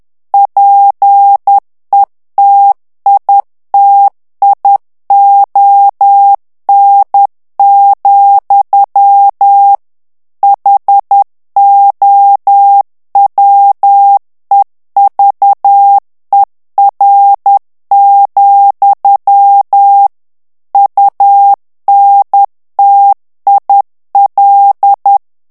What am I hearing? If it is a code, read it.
Morse code, heard as a keyed tone, about 10 wpm, 800 Hz. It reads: PETITION, HOWEVER, UNTIL
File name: ANGLAIS texte clair 3 vitesse 10 mots minute